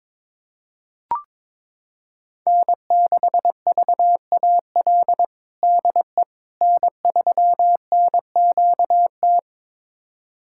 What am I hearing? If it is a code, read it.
Morse code, heard as a keyed tone, about 22 wpm, 700 Hz. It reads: N6VAL DE N3NQT
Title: USA Callsign DE Callsign - 5 Characters - 22wpm